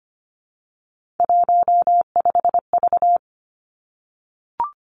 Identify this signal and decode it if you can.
Morse code, heard as a keyed tone, about 25 wpm, 700 Hz. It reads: 15V
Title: Three Challenging Letter-Number Combinations 25wpm